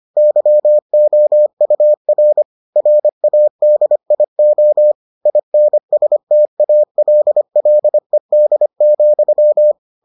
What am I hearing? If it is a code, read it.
Morse code, heard as a keyed tone, about 25 wpm, 600 Hz. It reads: YOUR RADIO INSTALLED,